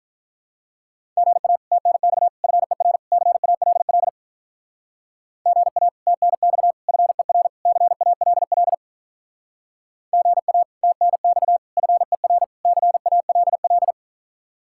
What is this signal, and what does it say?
Morse code, heard as a keyed tone, about 44 wpm, 700 Hz. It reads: GA TNX FER CALL GA TNX FER CALL GA TNX FER CALL